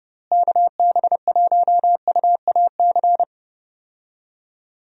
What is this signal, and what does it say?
Morse code, heard as a keyed tone, about 31 wpm, 700 Hz. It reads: KB1UAC
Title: USA Call Signs - 6 Characters - Repeated 3x - 30wpm